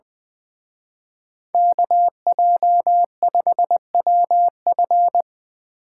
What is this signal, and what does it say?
Morse code, heard as a keyed tone, about 24 wpm, 700 Hz. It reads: KJ5WF
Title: Speed Racing - USA 2x2 Call Signs 20wpm